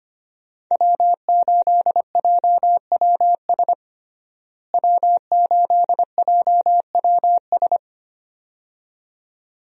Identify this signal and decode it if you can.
Morse code, heard as a keyed tone, about 25 wpm, 700 Hz. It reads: W8JWS W8JWS
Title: USA Call Signs - 5 Characters - Repeated 2x - 25wpm